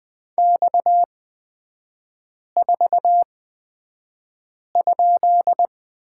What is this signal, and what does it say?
Morse code, heard as a keyed tone, about 20 wpm, spaced as for 5 wpm, 700 Hz. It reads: X4?
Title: 243 Lesson 39e - Letter z - Three Characters 20wpm